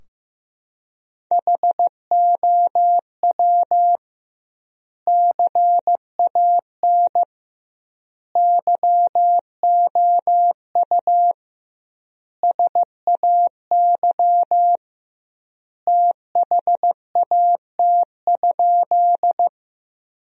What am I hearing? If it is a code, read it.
Morse code, heard as a keyed tone, about 15 wpm, 700 Hz. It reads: HOW CAN YOU SAY THAT?
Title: 2x Word-Spacing - Sentences from Top 200 Words - 15wpm